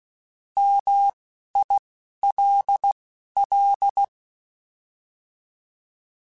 Morse code, heard as a keyed tone, about 16 wpm, spaced as for 11 wpm, 800 Hz. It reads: MILL